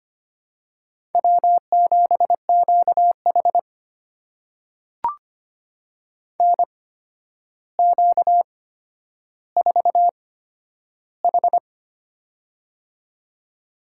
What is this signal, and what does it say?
Morse code, heard as a keyed tone, about 25 wpm, spaced as for 11 wpm, 700 Hz. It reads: W7QH NQ4H
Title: USA Call Signs - 8x Character Spacing - 25wpm